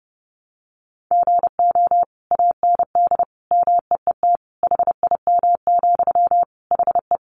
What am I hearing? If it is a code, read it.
Morse code, heard as a keyed tone, about 30 wpm, 700 Hz. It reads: GO AND MEET HIM, HE
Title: Sentences from Top 1100 Words 30wpm